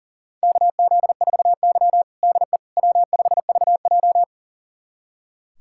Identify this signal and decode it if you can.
Morse code, heard as a keyed tone, about 40 wpm, 700 Hz. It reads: KZ4Y DE W5VJ
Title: USA Callsign DE Callsign - 4 Characters - 40wpm